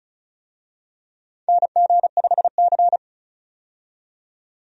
Morse code, heard as a keyed tone, about 34 wpm, 700 Hz. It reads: NG5C
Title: USA Call Signs - 4 Characters - 35wpm